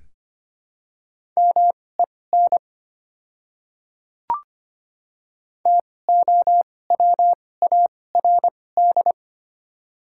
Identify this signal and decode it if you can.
Morse code, heard as a keyed tone, about 25 wpm, spaced as for 18 wpm, 700 Hz. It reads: MEN TOWARD
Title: 2x Character Spacing - Top 500 Words - 25wpm